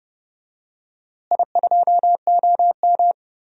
Morse code, heard as a keyed tone, about 31 wpm, 700 Hz. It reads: I2OM
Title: DXCC-100 Call Signs - 4 Characters - Repeated 3x - 30wpm